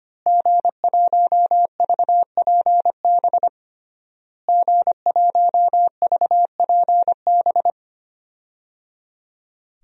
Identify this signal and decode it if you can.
Morse code, heard as a keyed tone, about 25 wpm, 700 Hz. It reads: G1VPB G1VPB
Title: DXCC-100 Call Signs - 5 Characters - Repeated 3x - 25wpm